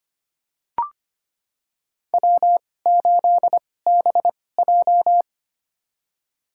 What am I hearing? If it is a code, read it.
Morse code, heard as a keyed tone, about 25 wpm, spaced as for 18 wpm, 700 Hz. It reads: W8BJ
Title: USA Call Signs - 2x Character Spacing - 25wpm